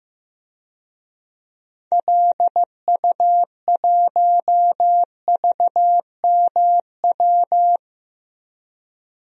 Morse code, heard as a keyed tone, about 15 wpm, 700 Hz. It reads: LU1VMW
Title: DXCC-100 Call Signs - 6 Characters - Repeated 3x - 15wpm